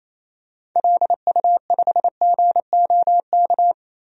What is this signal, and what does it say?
Morse code, heard as a keyed tone, about 28 wpm, 700 Hz. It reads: LU5GOK
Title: DXCC-100 Call Signs - 6 Characters - Repeated 3x - 28wpm